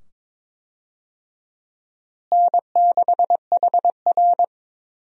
Morse code, heard as a keyed tone, about 21 wpm, 700 Hz. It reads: N6HR
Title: USA Call Signs - 4 Characters - Repeated 2x - 22wpm